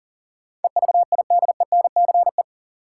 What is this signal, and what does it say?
Morse code, heard as a keyed tone, about 40 wpm, 700 Hz. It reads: EVIDENCE
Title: Sentences from Top 1300 Words - New Word Drill - 40wpm